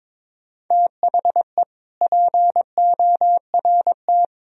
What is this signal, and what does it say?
Morse code, heard as a keyed tone, about 22 wpm, 700 Hz. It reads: THE PORT
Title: Sentences from Top 1000 Words - Review Part 1 - 22wpm